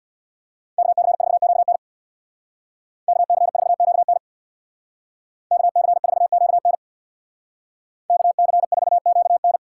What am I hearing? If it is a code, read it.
Morse code, heard as a keyed tone, about 64 wpm, 700 Hz. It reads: KC4QN KC4QN KC4QN KC4QN